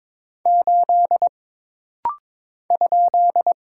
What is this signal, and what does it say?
Morse code, heard as a keyed tone, about 22 wpm, 700 Hz. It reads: 8 ?
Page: This rapid-fire timing is geared to push your ICR (Instant Character Recognition) skills.